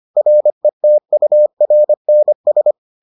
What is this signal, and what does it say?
Morse code, heard as a keyed tone, about 25 wpm, 600 Hz. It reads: RETURNS